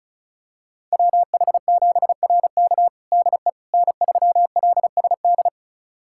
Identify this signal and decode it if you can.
Morse code, heard as a keyed tone, about 34 wpm, 700 Hz. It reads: WH7RK DE N3LSD
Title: USA Callsign DE Callsign - 5 Characters - 35wpm